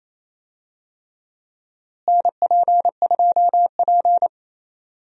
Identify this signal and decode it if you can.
Morse code, heard as a keyed tone, about 28 wpm, 700 Hz. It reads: NP2P
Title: USA Call Signs - 4 Characters - Repeated 3x - 28wpm